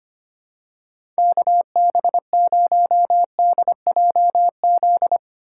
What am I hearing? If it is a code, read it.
Morse code, heard as a keyed tone, about 25 wpm, 700 Hz. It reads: KB0DJZ
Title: USA Call Signs - 2x Character Spacing - 25wpm